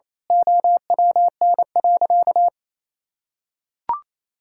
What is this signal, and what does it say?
Morse code, heard as a keyed tone, about 28 wpm, 700 Hz. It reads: OWN.
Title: Sentences from Top 1000 Words - Review Part 6 - 28wpm